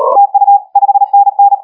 Morse code, heard as a keyed tone, about 35 wpm, 810 Hz. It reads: TU 5NN
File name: HG1R CW SSB MIX